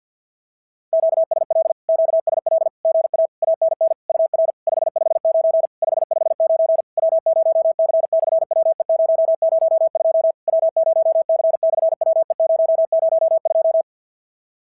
Morse code, heard as a keyed tone, about 50 wpm, 650 Hz. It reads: QSL QSL GA ANN UR 559 559 W0C/WE002 W0C/WE002
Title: SOTA QSO Practice 50wpm